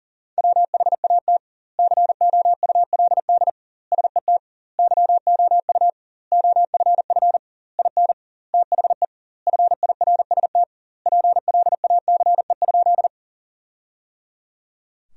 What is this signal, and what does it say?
Morse code, heard as a keyed tone, about 37 wpm, 700 Hz. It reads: WHAT COULD SET YOU OFF IN THE FIRST PLACE?
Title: 2x Word-Spacing - Sentences from Top 200 Words - 40wpm